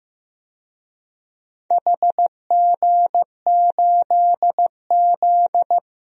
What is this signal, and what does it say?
Morse code, heard as a keyed tone, about 15 wpm, 700 Hz. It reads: HG8Z
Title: DXCC-100 Call Signs - 4 Characters - Repeated 3x - 15wpm